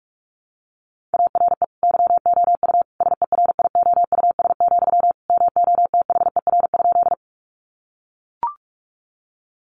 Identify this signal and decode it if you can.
Morse code, heard as a keyed tone, about 44 wpm, 700 Hz. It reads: ARE YOU SERIOUS, MOTHER?
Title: Sentences from Top 1100 Words 45wpm